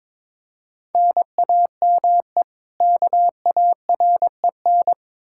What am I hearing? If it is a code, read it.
Morse code, heard as a keyed tone, about 22 wpm, 700 Hz. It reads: NAME KAREN
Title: Practice QSO - Exchange 1 - Beginning - 22wpm